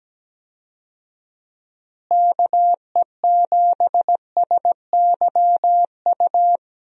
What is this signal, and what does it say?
Morse code, heard as a keyed tone, about 17 wpm, 700 Hz. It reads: KE7SYU
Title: USA Call Signs - 6 Characters - Repeated 2x - 17wpm